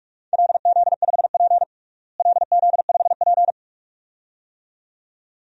Morse code, heard as a keyed tone, about 46 wpm, 700 Hz. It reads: LZ5P LZ5P
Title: DXCC-100 Call Signs - 4 Characters - Repeated 2x - 45wpm